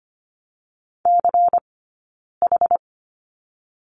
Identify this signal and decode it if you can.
Morse code, heard as a keyed tone, about 25 wpm, 700 Hz. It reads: C H
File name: course / 25:10wpm / 092 Lesson 16d - Letter c - Two Characters 25wpm